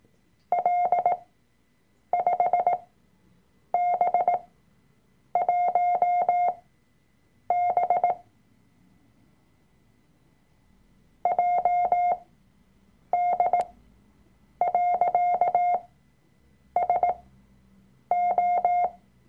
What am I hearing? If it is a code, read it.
Morse code, heard as a keyed tone, about 18 wpm, spaced as for 7 wpm, 710 Hz. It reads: L5B1B JD.SO
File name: Workshop Dateien / CW-Online-Kurs / Lektion 003